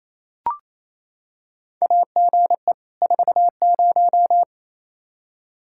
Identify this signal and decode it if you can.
Morse code, heard as a keyed tone, about 28 wpm, 700 Hz. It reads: AGE 40
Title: Practice QSO - Exchange 3 - More Info - 28wpm